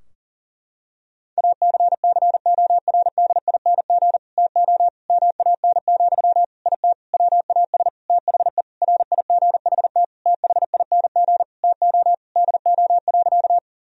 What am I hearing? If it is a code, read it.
Morse code, heard as a keyed tone, about 37 wpm, 700 Hz. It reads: ACCORDING TO MAN, IT WAS THE RIGHT THING TO DO.
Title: Sentences from Top 1000 Words - Review Part 4 - 40wpm